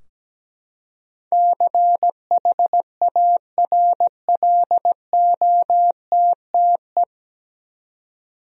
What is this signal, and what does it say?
Morse code, heard as a keyed tone, about 17 wpm, 700 Hz. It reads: CHARLOTTE
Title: USA Common Names - 17wpm